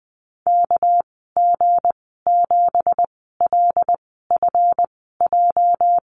Morse code, heard as a keyed tone, about 20 wpm, spaced as for 14 wpm, 700 Hz. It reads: KG7LFJ
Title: 231 Lesson 37g - Letter x - Call Signs 20wpm